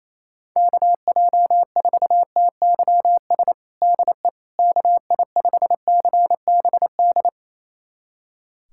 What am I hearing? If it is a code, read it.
Morse code, heard as a keyed tone, about 28 wpm, 700 Hz. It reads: KJ4TYS DE KI5CBD
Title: USA Callsign DE Callsign - 6 Characters - 28wpm